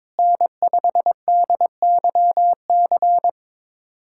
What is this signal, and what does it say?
Morse code, heard as a keyed tone, about 22 wpm, 700 Hz. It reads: N5DYC
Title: USA Callsign DE Callsign - 5 Characters - 22wpm